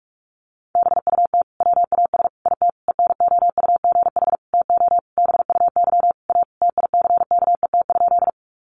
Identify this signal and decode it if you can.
Morse code, heard as a keyed tone, about 44 wpm, 700 Hz. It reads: BUT WAS IT ENOUGH TO BUY A TICKET?
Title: Sentences from Top 1400 Words 45wpm